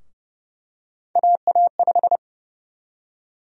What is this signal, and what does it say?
Morse code, heard as a keyed tone, about 30 wpm, 700 Hz. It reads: AA5
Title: USA Call Sign Prefixes - 30wpm